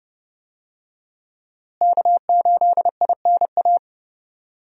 Morse code, heard as a keyed tone, about 30 wpm, 700 Hz. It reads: K8INA
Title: USA Call Signs - 5 Characters - Repeated 2x - 30wpm